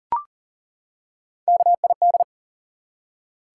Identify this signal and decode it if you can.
Morse code, heard as a keyed tone, about 38 wpm, 700 Hz. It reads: KID
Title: Three Letter Words - 40wpm